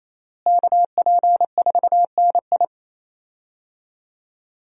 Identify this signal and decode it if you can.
Morse code, heard as a keyed tone, about 28 wpm, 700 Hz. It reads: KP4NI
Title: USA Call Signs - 5 Characters - Repeated 3x - 28wpm